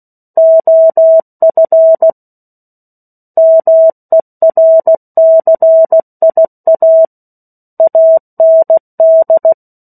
Morse code, heard as a keyed tone, about 16 wpm, 650 Hz. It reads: OF MERCIA AND
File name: 03-Chapter-03 16 WPM